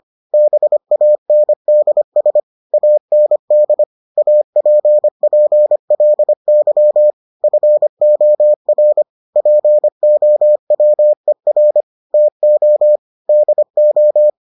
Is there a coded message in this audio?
Yes: Morse code from 0.3 to 14.3 s, heard as a keyed tone, about 25 wpm, 600 Hz. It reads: BANDS AND APPLY FOR POWER TO DO